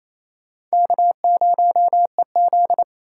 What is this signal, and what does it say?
Morse code, heard as a keyed tone, about 35 wpm, 700 Hz. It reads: K0EZ
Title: Speed Racing - USA 1x2 Call Signs 28wpm